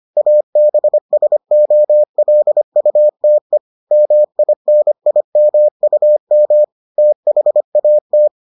Morse code, heard as a keyed tone, about 25 wpm, 600 Hz. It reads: ABSOLUTE MINIMUM THAT